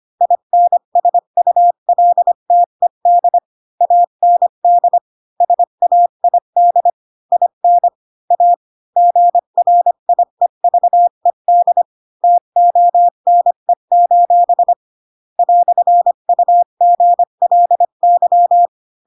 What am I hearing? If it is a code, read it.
Morse code, heard as a keyed tone, about 25 wpm, 700 Hz. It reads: INSULTED AND SAID IN A GRIEVED TONE: "UGLY
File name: Pinocchio / Pinocchio ch3 20 25 ik1hge